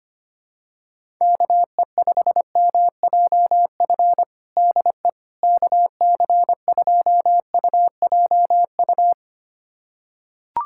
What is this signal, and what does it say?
Morse code, heard as a keyed tone, about 25 wpm, 700 Hz. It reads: KE5MJF DE KC2UJU
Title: USA Callsign DE Callsign - 6 Characters - 25wpm